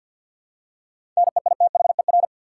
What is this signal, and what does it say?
Morse code, heard as a keyed tone, about 52 wpm, 700 Hz. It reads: NEITHER
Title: Seven Letter Words - 50wpm